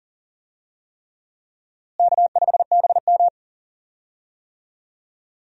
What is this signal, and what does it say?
Morse code, heard as a keyed tone, about 40 wpm, 700 Hz. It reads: K5BM
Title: USA Call Signs - 4 Characters - Repeated 3x - 40wpm